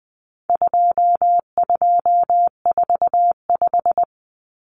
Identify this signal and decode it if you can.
Morse code, heard as a keyed tone, about 20 wpm, 700 Hz. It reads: 2245
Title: Numbers - 4-Digit 20wpm